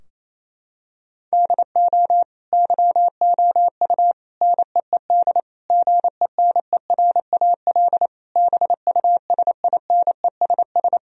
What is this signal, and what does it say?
Morse code, heard as a keyed tone, about 28 wpm, 700 Hz. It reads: DO YOU NEED GENERAL BUSINESS
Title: Sentences from Top 1000 Words - Review Part 3 - 28wpm